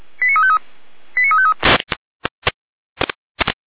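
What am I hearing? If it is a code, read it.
Five-tone selective call (ZVEI), heard as quick runs of five tones, tones of about 70 ms. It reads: 89353 89353
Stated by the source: ZVEI five-tone selective call received by SDR.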